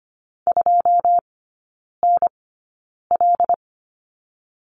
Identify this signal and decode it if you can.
Morse code, heard as a keyed tone, about 25 wpm, spaced as for 9 wpm, 690 Hz. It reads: 2NL
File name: Lesson 14e - Number 2 - Three Characters 25wpm